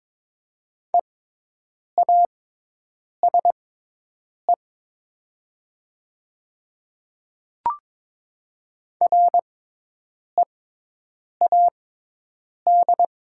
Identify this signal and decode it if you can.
Morse code, heard as a keyed tone, about 22 wpm, spaced as for 8 wpm, 700 Hz. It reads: EASE READ
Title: 6x Character Spacing - Top 300 Words - 22wpm